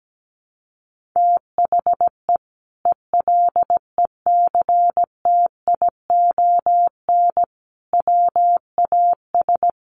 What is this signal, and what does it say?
Morse code, heard as a keyed tone, about 17 wpm, 700 Hz. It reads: THE ELECTION WAS
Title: Sentences from Top 1500 Words 17wpm